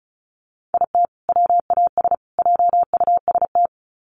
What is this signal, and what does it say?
Morse code, heard as a keyed tone, about 34 wpm, 700 Hz. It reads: IT WAS JUST
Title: Sets of 3 Words 35wpm